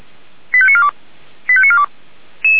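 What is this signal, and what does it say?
Five-tone selective call (ZVEI), heard as quick runs of five tones, tones of about 70 ms. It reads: 85942 85942
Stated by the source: ZVEI five-tone selective call received on SDR.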